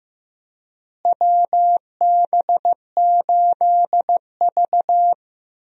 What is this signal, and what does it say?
Morse code, heard as a keyed tone, about 15 wpm, 700 Hz. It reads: WB8V
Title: USA Call Signs - 4 Characters - Repeated 3x - 15wpm